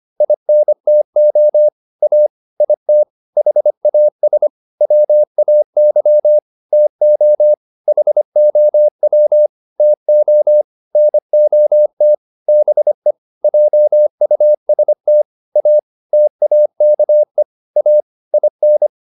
Morse code, heard as a keyed tone, about 25 wpm, 600 Hz. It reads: INTO A IT HAS WAY TO HOW TO NOT BE JUST A TAKE A IN